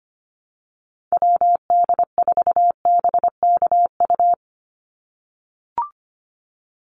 Morse code, heard as a keyed tone, about 25 wpm, 700 Hz. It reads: WD4BKU
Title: USA Call Signs 25wpm